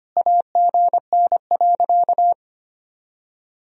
Morse code, heard as a keyed tone, about 25 wpm, 700 Hz. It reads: AGN.
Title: Sentences from Top 300 Words and Abbreviations 25wpm